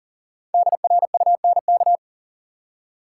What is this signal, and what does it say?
Morse code, heard as a keyed tone, about 40 wpm, 700 Hz. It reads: DRUNK
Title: Five Letter Words - 40wpm